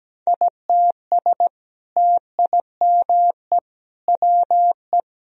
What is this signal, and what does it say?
Morse code, heard as a keyed tone, about 17 wpm, 700 Hz. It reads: ITS TIME WE
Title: Sentences from Top 1000 Words - Review Part 4 - 17wpm